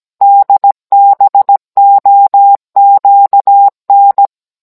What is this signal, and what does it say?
Morse code, heard as a keyed tone, about 17 wpm, 800 Hz. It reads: DBOQN